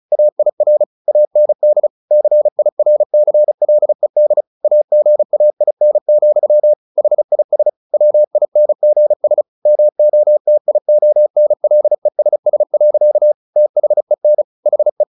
Morse code, heard as a keyed tone, about 34 wpm, 600 Hz. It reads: AIR AND CIRCLED AGAIN, HIS WINGS MOTIONLESS. THEN HE
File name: the old man and the sea / the old man and the sea 1 35wpm